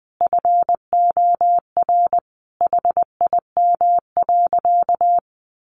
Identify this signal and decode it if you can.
Morse code, heard as a keyed tone, about 20 wpm, 700 Hz. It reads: FOR HIM.
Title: Sentences from Top 500 Words 20wpm